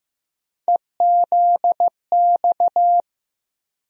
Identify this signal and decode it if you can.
Morse code, heard as a keyed tone, about 15 wpm, 700 Hz. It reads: EZX
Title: USA Call Sign Postfixes - 3 Letters 15wpm